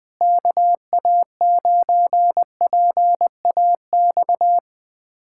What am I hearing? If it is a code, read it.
Morse code, heard as a keyed tone, about 20 wpm, 700 Hz. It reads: KA9PAX
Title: USA Callsign DE Callsign - 6 Characters - 20wpm